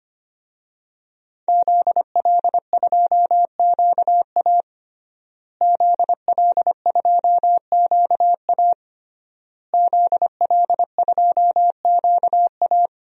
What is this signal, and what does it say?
Morse code, heard as a keyed tone, about 25 wpm, 700 Hz. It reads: ZL2QA ZL2QA ZL2QA
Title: DXCC-100 Call Signs - 5 Characters - Repeated 3x - 25wpm